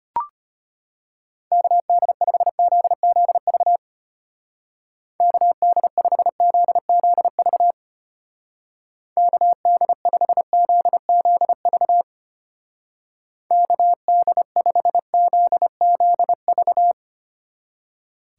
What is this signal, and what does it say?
Morse code, heard as a keyed tone, about 31 wpm, 700 Hz. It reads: KD5ZZV KD5ZZV KD5ZZV KD5ZZV
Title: Speed Racing - USA 2x3 Call Signs 25wpm